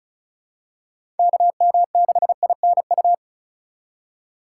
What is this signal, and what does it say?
Morse code, heard as a keyed tone, about 36 wpm, 700 Hz. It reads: KM6INU
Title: USA Call Signs - 3x Character Spacing - 35wpm